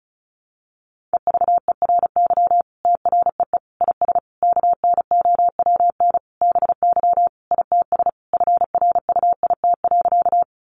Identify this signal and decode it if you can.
Morse code, heard as a keyed tone, about 34 wpm, 700 Hz. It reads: EVERY TREE IS KNOWN BY ITS FRUIT.
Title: Sentences from Top 1300 Words 35wpm